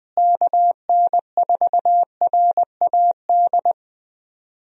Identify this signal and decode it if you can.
Morse code, heard as a keyed tone, about 20 wpm, 700 Hz. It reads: KN4RAD
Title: USA Call Signs - 6 Characters - Repeated 2x - 20wpm